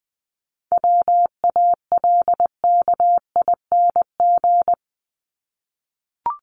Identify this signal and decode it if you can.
Morse code, heard as a keyed tone, about 20 wpm, 700 Hz. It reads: WALKING
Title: ING Suffix - Top 100 Words with Base Words 20wpm